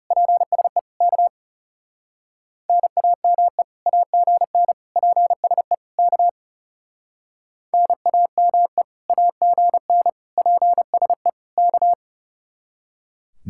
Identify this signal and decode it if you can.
Morse code, heard as a keyed tone, about 38 wpm, 700 Hz. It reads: PSE K NAME AGN PSE K NAME AGN PSE K